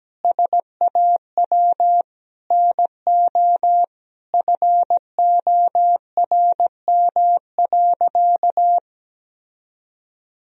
Morse code, heard as a keyed tone, about 17 wpm, 700 Hz. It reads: SAW NO FORM.
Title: Sentences from Top 1000 Words - Review Part 1 - 17wpm